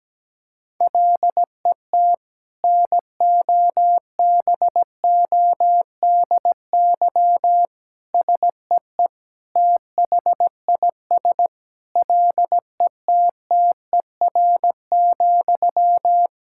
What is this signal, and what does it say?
Morse code, heard as a keyed tone, about 17 wpm, 700 Hz. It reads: LET NOBODY SEE THIS LETTER,
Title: Sentences from Top 1600 Words 17wpm